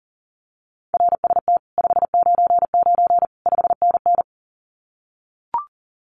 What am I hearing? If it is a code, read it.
Morse code, heard as a keyed tone, about 40 wpm, 700 Hz. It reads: RST 599 5NN
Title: Common QSO Phrases Encore 40wpm